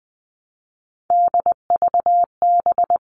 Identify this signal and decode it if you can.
Morse code, heard as a keyed tone, about 20 wpm, 700 Hz. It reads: DVB
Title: All Cut Numbers - 3-Digit 20wpm